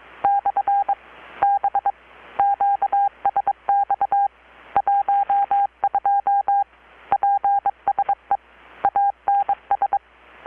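Morse code, heard as a keyed tone, about 23 wpm, 800 Hz. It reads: / B QSX 12 PSE ANS
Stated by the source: Morse from KPH's Night of Nights XX maritime radiotelegraphy event, received in Chiapas, Mexico about 3,800 km away on a Kenwood R-600 with a grounded 10 m long-wire antenna.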